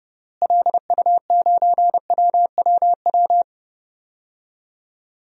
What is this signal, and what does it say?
Morse code, heard as a keyed tone, about 32 wpm, 700 Hz. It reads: LU9WWW
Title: DXCC-100 Call Signs - 6 Characters - Repeated 2x - 30wpm